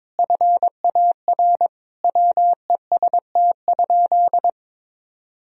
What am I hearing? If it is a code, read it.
Morse code, heard as a keyed tone, about 22 wpm, 700 Hz. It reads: FAR WEST?
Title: Sentences from Top 1000 Words - Review Part 2 - 22wpm